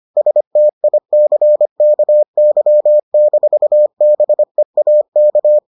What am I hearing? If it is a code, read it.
Morse code, heard as a keyed tone, about 25 wpm, 600 Hz. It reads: STICKY-BEAK